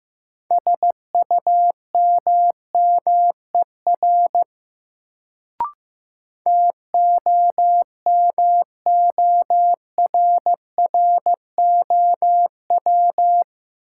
Morse code, heard as a keyed tone, about 15 wpm, 700 Hz. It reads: SUMMER TOMORROW
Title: Sentences from Top 1100 Words - New Word Drill Alt - 15wpm